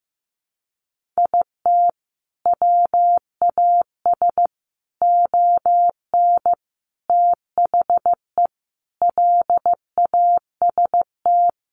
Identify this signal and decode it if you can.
Morse code, heard as a keyed tone, about 15 wpm, 700 Hz. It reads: IT WAS ON THE LAST
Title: Sentences from Top 1200 Words 15wpm